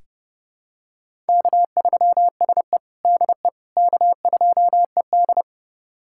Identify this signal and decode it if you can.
Morse code, heard as a keyed tone, about 32 wpm, 700 Hz. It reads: K3SE DE K2ED
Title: USA Callsign DE Callsign - 4 Characters - 30wpm